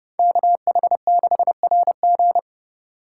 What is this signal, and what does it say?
Morse code, heard as a keyed tone, about 28 wpm, 700 Hz. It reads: KH6RG